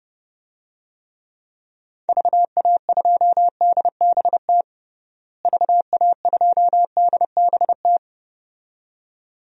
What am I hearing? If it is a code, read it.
Morse code, heard as a keyed tone, about 30 wpm, 700 Hz. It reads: VA2DBT VA2DBT
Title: DXCC-100 Call Signs - 6 Characters - Repeated 2x - 30wpm